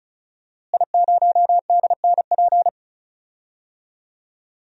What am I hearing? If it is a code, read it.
Morse code, heard as a keyed tone, about 34 wpm, 700 Hz. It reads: I0DNP
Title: DXCC-100 Call Signs - 5 Characters - Repeated 3x - 35wpm